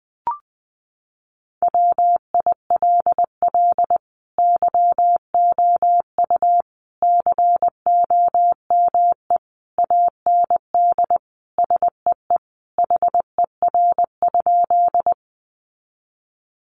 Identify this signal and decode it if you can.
Morse code, heard as a keyed tone, about 20 wpm, 700 Hz. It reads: WILL YOU COME AND SEE HER?
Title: Sentences from Top 100 Words 20wpm